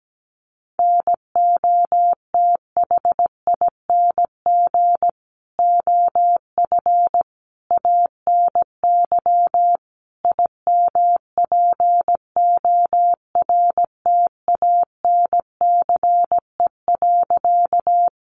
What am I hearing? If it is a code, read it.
Morse code, heard as a keyed tone, about 17 wpm, 700 Hz. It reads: NOTHING OF ANY IMPORTANCE.
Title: Sentences from Top 1200 Words 17wpm